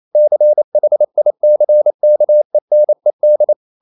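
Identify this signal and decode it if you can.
Morse code, heard as a keyed tone, about 28 wpm, 600 Hz. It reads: CHICKENED
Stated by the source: Morse code practice at 28 wpm, text of Ken Follett's novel Never in English, from CW med Gnister.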